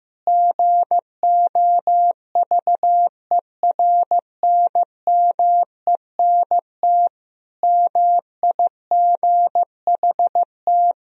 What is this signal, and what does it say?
Morse code, heard as a keyed tone, about 15 wpm, 700 Hz. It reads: GOVERNMENT MIGHT
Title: Sentences from Top 1000 Words - Review Part 3 - 15wpm